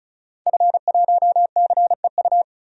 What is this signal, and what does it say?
Morse code, heard as a keyed tone, about 34 wpm, 700 Hz. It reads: F1CEU
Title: DXCC-100 Call Signs - 5 Characters - 35wpm